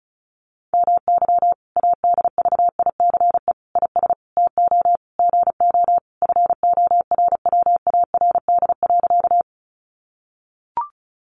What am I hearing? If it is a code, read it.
Morse code, heard as a keyed tone, about 34 wpm, 700 Hz. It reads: MY ADVICE IS TO GO FORWARD.
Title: Sentences from Top 1300 Words 35wpm